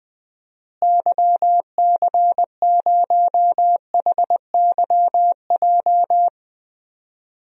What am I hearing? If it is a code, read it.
Morse code, heard as a keyed tone, about 21 wpm, 700 Hz. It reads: YC0HYJ
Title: DXCC-100 Call Signs - 6 Characters - Repeated 2x - 20wpm